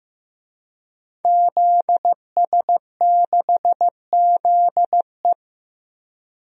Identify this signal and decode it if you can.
Morse code, heard as a keyed tone, about 15 wpm, 700 Hz. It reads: ZS6ZE
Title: DXCC-100 Call Signs - 5 Characters - 15wpm